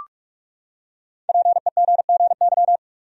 Morse code, heard as a keyed tone, about 46 wpm, 700 Hz. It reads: PEGGY
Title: USA Common Names - 45wpm